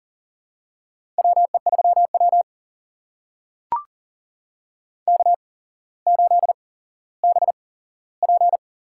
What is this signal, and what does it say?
Morse code, heard as a keyed tone, about 40 wpm, spaced as for 17 wpm, 700 Hz. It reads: WE3W K8BP
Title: USA Call Signs - 4 Characters - 8x Character Spacing - 40wpm